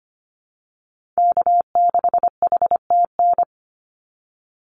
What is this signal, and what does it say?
Morse code, heard as a keyed tone, about 25 wpm, 700 Hz. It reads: K6HTN
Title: Call Signs Repeated 3x - 25wpm